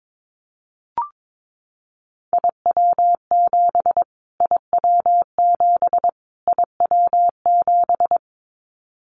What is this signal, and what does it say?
Morse code, heard as a keyed tone, about 22 wpm, 700 Hz. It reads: IW7 IW7 IW7
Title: Callsign Prefixes for Top 25 DXCC - Repeated 3x - 22wpm